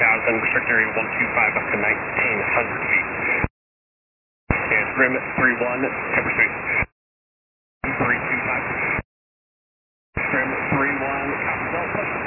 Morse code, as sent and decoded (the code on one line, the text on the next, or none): none